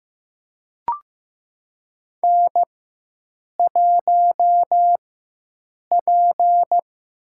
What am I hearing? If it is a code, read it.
Morse code, heard as a keyed tone, about 15 wpm, spaced as for 7 wpm, 700 Hz. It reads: N1P
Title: USA Call Signs - 4x Character Spacing - 15wpm